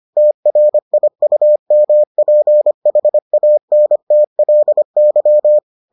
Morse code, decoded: TRIUMPHANTLY

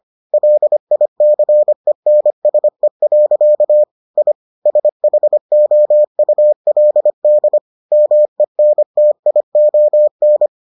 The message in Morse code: LICENSE. I SHOULD MENTION

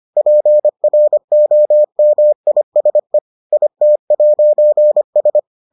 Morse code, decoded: PROMISE IT'S

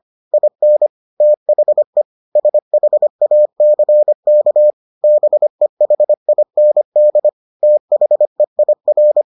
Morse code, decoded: IN THE SHACK BEHIND THEIR

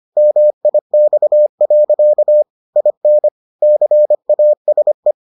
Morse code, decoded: MIX. IN CASE

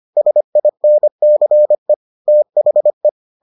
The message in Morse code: SINCE THE

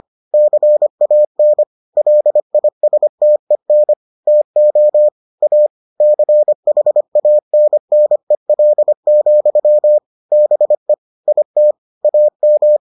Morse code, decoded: CAN LISTEN TO A CHANNEL, BE IT AM

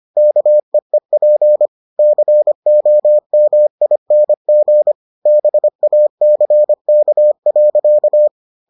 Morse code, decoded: KEEP COMING BACK.

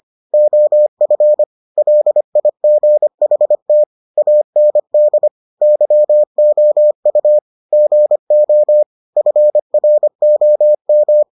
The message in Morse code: OF LIGHT AND YOU GO FROM

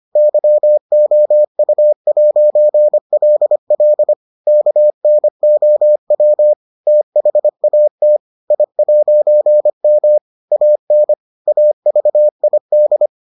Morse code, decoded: YOU'LL KNOW THAT I'M AN AVID

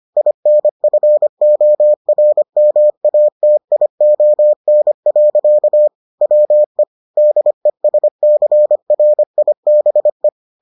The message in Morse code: INFORMATION. WE DESCRIBE

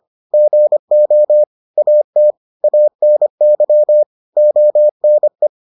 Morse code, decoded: GO AT ANY ONE

